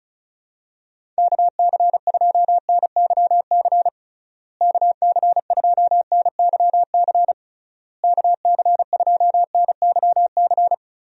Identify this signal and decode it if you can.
Morse code, heard as a keyed tone, about 34 wpm, 700 Hz. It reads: KC2NYC KC2NYC KC2NYC
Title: USA Call Signs - 6 Characters - Repeated 3x - 35wpm